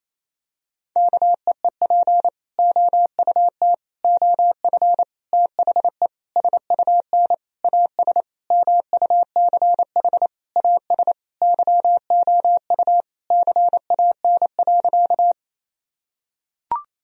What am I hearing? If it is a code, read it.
Morse code, heard as a keyed tone, about 28 wpm, 700 Hz. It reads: KEEP OUT OF THE SUN AS MUCH AS YOU CAN.
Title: Sentences from Top 1000 Words - Review Part 1 - 28wpm